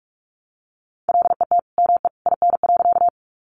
Morse code, heard as a keyed tone, about 44 wpm, 700 Hz. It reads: LET ME IN.
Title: Sentences from Top 300 Words 45wpm